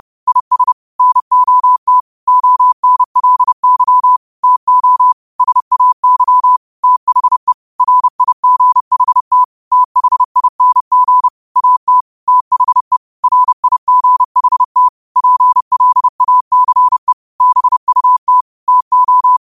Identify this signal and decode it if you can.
Morse code, heard as a keyed tone, about 30 wpm, 1000 Hz. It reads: IS NOT ONLY TO SAY THE RIGHT THING AT THE RIGHT PLACE BUT TO